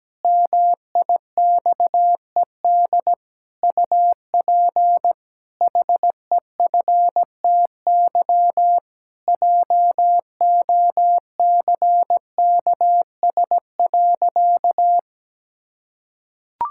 Morse code, heard as a keyed tone, about 17 wpm, 700 Hz. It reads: MIXED UP HEFTY JOCKS.
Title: Pangram Challenge v2 - 17wpm